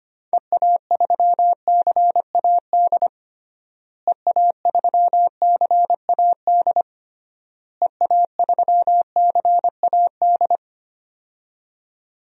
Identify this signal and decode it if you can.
Morse code, heard as a keyed tone, about 25 wpm, 700 Hz. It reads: EA3CAD EA3CAD EA3CAD
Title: DXCC-100 Call Signs - 6 Characters - Repeated 3x - 25wpm